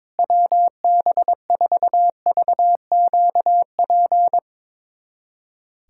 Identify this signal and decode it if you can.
Morse code, heard as a keyed tone, about 22 wpm, 700 Hz. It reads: WB4VQP
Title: USA Callsign DE Callsign - 6 Characters - 22wpm